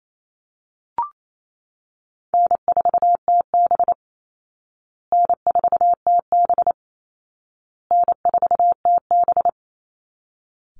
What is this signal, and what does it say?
Morse code, heard as a keyed tone, about 28 wpm, 700 Hz. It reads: N4TB N4TB N4TB
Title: Call Signs Repeated 2x - 28wpm